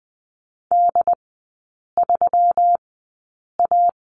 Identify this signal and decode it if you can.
Morse code, heard as a keyed tone, about 20 wpm, spaced as for 8 wpm, 700 Hz. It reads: D3A
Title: 121 Lesson 20e - Number 3 - Three Characters 20wpm